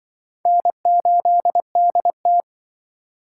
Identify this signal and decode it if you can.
Morse code, heard as a keyed tone, about 25 wpm, 700 Hz. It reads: N8DT
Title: Speed Racing - USA 1x2 Call Signs 20wpm